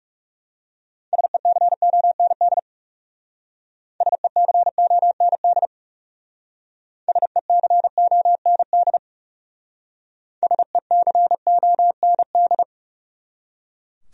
Morse code, heard as a keyed tone, about 37 wpm, 700 Hz. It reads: SECOND SECOND SECOND SECOND
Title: Speed-Racing Top 300 Words 30wpm